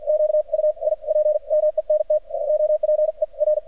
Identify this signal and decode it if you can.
Morse code, heard as a keyed tone, about 53 wpm, spaced as for 40 wpm, 610 Hz, with radio noise. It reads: QUIPMENT OPER
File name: archived / CRSnet